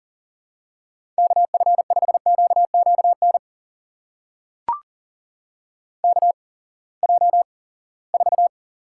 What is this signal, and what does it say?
Morse code, heard as a keyed tone, about 41 wpm, spaced as for 23 wpm, 700 Hz. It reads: KF5QQN KJ4
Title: USA Call Signs - 6 Characters - 8x Character Spacing - 40wpm